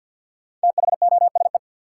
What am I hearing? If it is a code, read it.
Morse code, heard as a keyed tone, about 53 wpm, 700 Hz. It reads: THOSE
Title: Sentences from Top 300 Words - New Word Drill - 50wpm